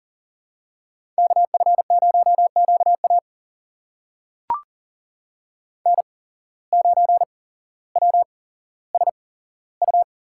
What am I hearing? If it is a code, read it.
Morse code, heard as a keyed tone, about 40 wpm, spaced as for 17 wpm, 700 Hz. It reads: KF0QA N9WSU
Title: USA Call Signs - 5 Characters - 8x Character Spacing - 40wpm